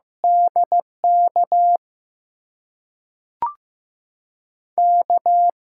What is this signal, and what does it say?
Morse code, heard as a keyed tone, about 15 wpm, 700 Hz. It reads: DK K
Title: USA Call Signs - 6 Characters - 8x Character Spacing - 15wpm